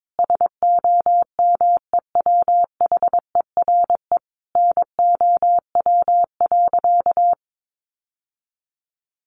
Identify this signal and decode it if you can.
Morse code, heard as a keyed tone, about 22 wpm, 700 Hz. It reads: SOMEWHERE NOW.
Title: Sentences from Top 1400 Words 22wpm